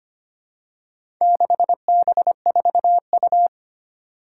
Morse code, heard as a keyed tone, about 25 wpm, 700 Hz. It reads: 6B4U
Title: Four Challenging Letter-Number Combinations 25wpm